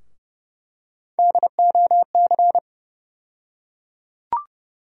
Morse code, heard as a keyed tone, about 30 wpm, 700 Hz. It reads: DOC
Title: USA Call Sign Postfixes - 3 Letters 30wpm